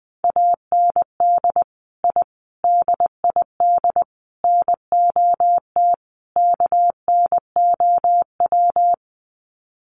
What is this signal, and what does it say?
Morse code, heard as a keyed tone, about 20 wpm, 700 Hz. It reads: AND I DID NOT KNOW
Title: Sets of 5 Words Encore 20wpm